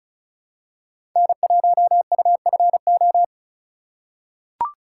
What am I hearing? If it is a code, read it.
Morse code, heard as a keyed tone, about 36 wpm, 700 Hz. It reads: N1UFO
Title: USA Call Signs - 3x Character Spacing - 35wpm